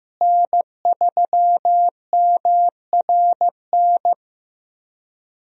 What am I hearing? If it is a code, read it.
Morse code, heard as a keyed tone, about 14 wpm, 700 Hz. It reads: N3MRN
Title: USA Call Signs - 5 Characters - Repeated 3x - 15wpm